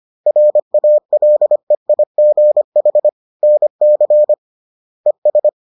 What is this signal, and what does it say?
Morse code, heard as a keyed tone, about 25 wpm, 600 Hz. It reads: RALEIGH NC ES